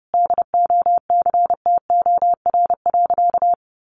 Morse code, heard as a keyed tone, about 30 wpm, 700 Hz. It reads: DOCTOR.
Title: Sentences from Top 1500 Words 30wpm